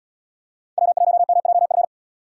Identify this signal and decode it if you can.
Morse code, heard as a keyed tone, about 77 wpm, 700 Hz. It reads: W1MJF